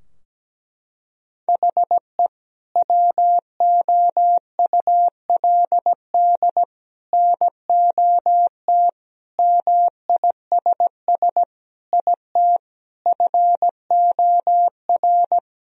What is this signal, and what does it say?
Morse code, heard as a keyed tone, about 17 wpm, 700 Hz. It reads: HE WOULD NOT MISS IT FOR
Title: Sentences from Top 1000 Words - Review Part 3 - 17wpm